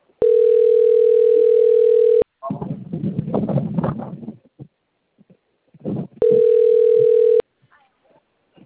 Ringback tone (North America), 440 + 480 Hz.
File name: out- - - - - .29823